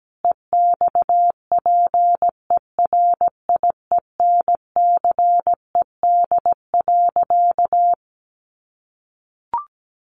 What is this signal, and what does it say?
Morse code, heard as a keyed tone, about 17 wpm, 700 Hz. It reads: EXPERIENCED.
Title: Sentences from Top 1100 Words 17wpm